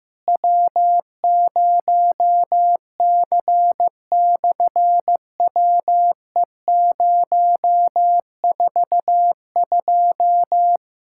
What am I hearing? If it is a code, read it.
Morse code, heard as a keyed tone, about 15 wpm, 700 Hz. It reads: W0C/WE042
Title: SOTA QSO Practice 15wpm